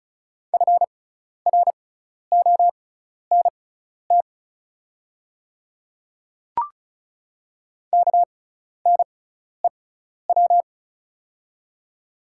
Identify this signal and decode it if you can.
Morse code, heard as a keyed tone, about 34 wpm, spaced as for 12 wpm, 690 Hz. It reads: FRONT KNEW
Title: 6x Character Spacing - Top 500 Words - 35wpm